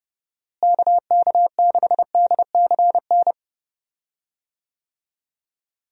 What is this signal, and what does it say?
Morse code, heard as a keyed tone, about 30 wpm, 700 Hz. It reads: KK6DCN
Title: USA Call Signs - 6 Characters - Repeated 3x - 30wpm